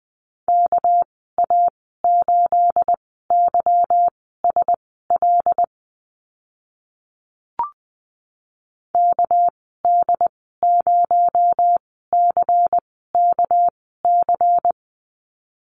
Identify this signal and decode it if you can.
Morse code, heard as a keyed tone, about 20 wpm, spaced as for 14 wpm, 700 Hz. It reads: KA8YSL KD0CKC
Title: 219 Lesson 35g - Number 0 - Call Signs 20wpm